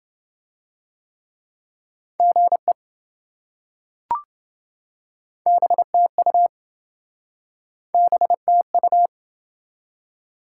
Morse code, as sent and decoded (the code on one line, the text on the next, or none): --. / -... - ..- / -... - ..-
GE BTU BTU